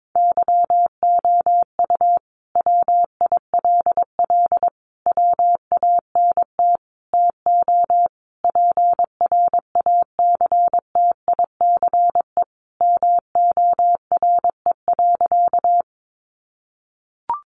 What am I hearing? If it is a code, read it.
Morse code, heard as a keyed tone, about 22 wpm, 700 Hz. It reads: YOU WILL WANT TO PRACTICE MORE.
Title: Sentences from Top 700 Words 22wpm